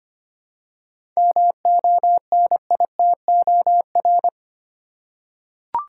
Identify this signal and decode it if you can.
Morse code, heard as a keyed tone, about 25 wpm, 700 Hz. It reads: MONITOR
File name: practice / 25wpm / Easy Words with Hints - 25wpm